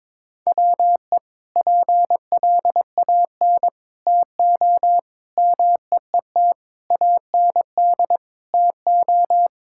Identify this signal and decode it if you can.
Morse code, heard as a keyed tone, about 22 wpm, 700 Hz. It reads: WE PLAN TO MEET AND TO
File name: practice / 22wpm / Sentences with 2-Word Phrases - 22wpm